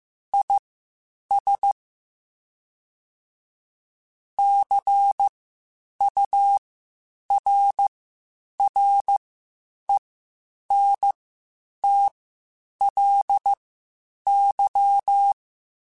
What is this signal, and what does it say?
Morse code, heard as a keyed tone, about 15 wpm, spaced as for 8 wpm, 800 Hz. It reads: IS CURRENTLY
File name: Spaced Characters / Spaced Characters-mp3s / Weather5 1 sp @13